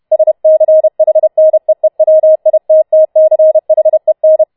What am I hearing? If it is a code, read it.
Morse code, heard as a keyed tone, about 31 wpm, 630 Hz. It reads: SCHNEEWITTCHEN